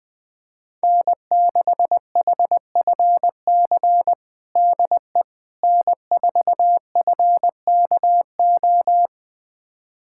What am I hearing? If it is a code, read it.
Morse code, heard as a keyed tone, about 21 wpm, 700 Hz. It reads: N6HFC DE N4FKO
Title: USA Callsign DE Callsign - 5 Characters - 20wpm